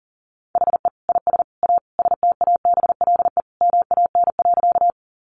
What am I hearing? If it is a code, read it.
Morse code, heard as a keyed tone, about 40 wpm, 700 Hz. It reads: HE IS A STABLE MAN.
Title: Sentences from Top 1500 Words 40wpm